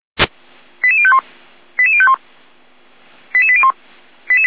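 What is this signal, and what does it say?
Five-tone selective call (ZVEI), heard as quick runs of five tones, tones of about 70 ms. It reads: 88051 88051 88801
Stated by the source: ZVEI five-tone selective call 88127 received via SDR.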